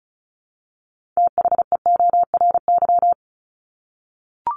Morse code, heard as a keyed tone, about 34 wpm, 700 Hz. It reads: THEORY
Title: Top 501-750 Words 35wpm